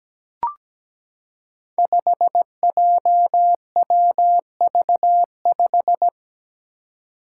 Morse code, heard as a keyed tone, about 17 wpm, 700 Hz. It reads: 5JWV5
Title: Five Challenging Letter-Number Combinations 17wpm